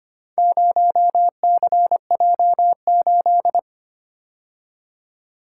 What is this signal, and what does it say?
Morse code, heard as a keyed tone, about 25 wpm, 700 Hz. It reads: 0CJ8
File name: practice / 25wpm / Four Letter-Number Combinations - 25wpm